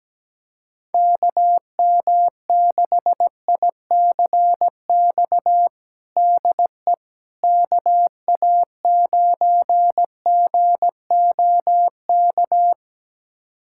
Morse code, heard as a keyed tone, about 17 wpm, 700 Hz. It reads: KM6ICX DE KA9GOK
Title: USA Callsign DE Callsign - 6 Characters - 17wpm